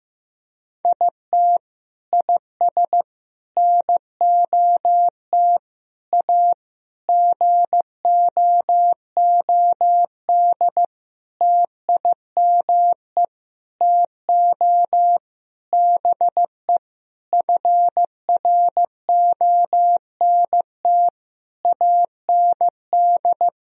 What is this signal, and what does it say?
Morse code, heard as a keyed tone, about 14 wpm, 700 Hz. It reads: IT IS NOT A GOOD TIME TO BE FRONT AND
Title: Sentences from Top 1000 Words - Review Part 2 - 15wpm